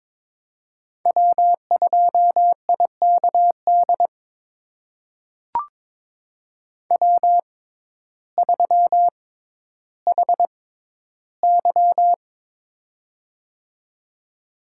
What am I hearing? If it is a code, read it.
Morse code, heard as a keyed tone, about 21 wpm, spaced as for 12 wpm, 700 Hz. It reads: W2IKD W3HY
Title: USA Call Signs - 6x Character Spacing - 22wpm